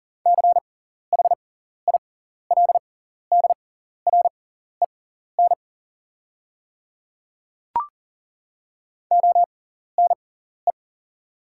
Morse code, heard as a keyed tone, about 40 wpm, spaced as for 14 wpm, 700 Hz. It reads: CHILDREN ONE
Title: 6x Character Spacing - Top 300 Words - 40wpm